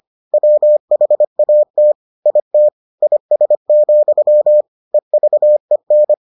Morse code, heard as a keyed tone, about 25 wpm, 600 Hz. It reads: WHAT IT IS, EVEN